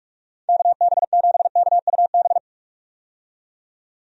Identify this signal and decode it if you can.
Morse code, heard as a keyed tone, about 46 wpm, 700 Hz. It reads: KB7KUB
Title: USA Call Signs - 6 Characters - 45wpm